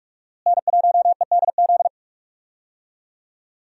Morse code, heard as a keyed tone, about 44 wpm, 700 Hz. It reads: N1EDZ
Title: USA Call Signs - 5 Characters - Repeated 3x - 45wpm